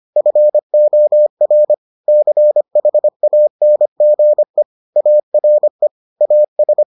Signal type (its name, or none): Morse code